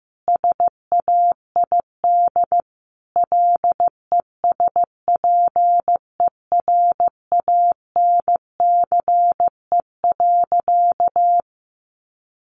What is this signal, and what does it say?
Morse code, heard as a keyed tone, about 14 wpm, 700 Hz. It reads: SAID LESPERANCE.
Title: A Sound Of Thunder 15wpm